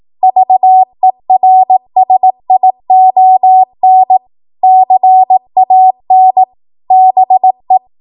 Morse code, heard as a keyed tone, about 18 wpm, 750 Hz. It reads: VERSION CAN BE
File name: Morse / 18WPM